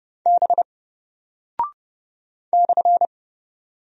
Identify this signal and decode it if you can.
Morse code, heard as a keyed tone, about 30 wpm, 700 Hz. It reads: B /
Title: Single Character (Letter:Number:Punctuation:Prosign) Alt 30wpm